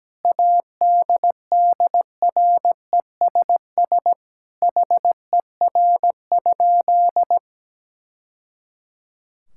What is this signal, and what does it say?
Morse code, heard as a keyed tone, about 17 wpm, 700 Hz. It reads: ADDRESS HER?
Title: Sentences from Top 1000 Words - Review Part 6 - 17wpm